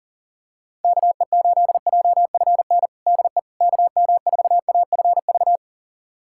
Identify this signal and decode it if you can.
Morse code, heard as a keyed tone, about 40 wpm, 700 Hz. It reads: KE8JFN DE KM4AFV
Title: USA Callsign DE Callsign - 6 Characters - 40wpm